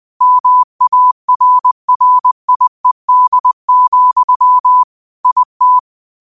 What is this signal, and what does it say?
Morse code, heard as a keyed tone, about 20 wpm, 1000 Hz. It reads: MARRIED, IT
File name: qotdfiles / 20wpm